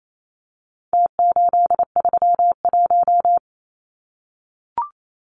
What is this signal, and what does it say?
Morse code, heard as a keyed tone, about 28 wpm, 700 Hz. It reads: T831
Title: Numbers - Contest Style - 4-Digit 28wpm